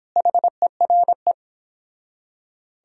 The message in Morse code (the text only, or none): HERE